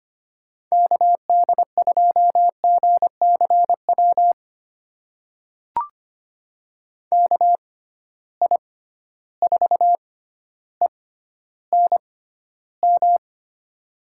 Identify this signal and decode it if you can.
Morse code, heard as a keyed tone, about 25 wpm, spaced as for 13 wpm, 700 Hz. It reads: KD2GCW KI4ENM
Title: USA Call Signs - 6 Characters - 6x Character Spacing - 25wpm